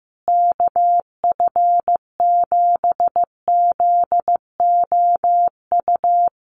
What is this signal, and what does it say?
Morse code, heard as a keyed tone, about 15 wpm, 700 Hz. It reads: KF7ZOU
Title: USA Call Signs Repeated 3x 15wpm